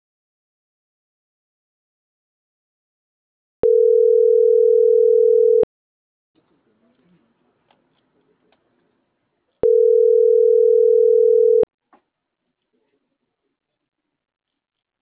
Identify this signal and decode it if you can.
Ringback tone (North America), 440 + 480 Hz.